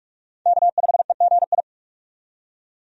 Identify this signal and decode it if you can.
Morse code, heard as a keyed tone, about 46 wpm, 700 Hz. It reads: K5EGI
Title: USA Call Signs - 5 Characters - 45wpm